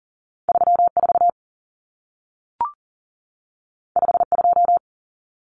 Morse code, heard as a keyed tone, about 40 wpm, 700 Hz. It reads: 34 52
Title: Numbers - 2-Digit 40wpm